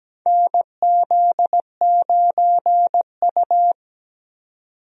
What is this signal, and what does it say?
Morse code, heard as a keyed tone, about 17 wpm, 700 Hz. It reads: NZ9U
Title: USA Call Signs - 4 Characters - 17wpm